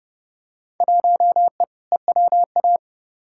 Morse code, heard as a keyed tone, about 30 wpm, 700 Hz. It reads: 1E EWA
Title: Field Day Exchanges - 30wpm